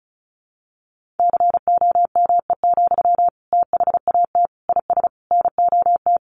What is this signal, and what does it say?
Morse code, heard as a keyed tone, about 34 wpm, 700 Hz. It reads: COME, THAT IS NOT
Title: Sentences from Top 700 Words 35wpm